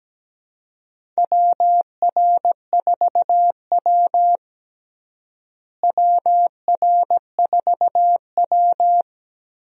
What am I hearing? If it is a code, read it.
Morse code, heard as a keyed tone, about 17 wpm, 700 Hz. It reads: WR4W WR4W
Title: USA Call Signs - 4 Characters - Repeated 3x - 17wpm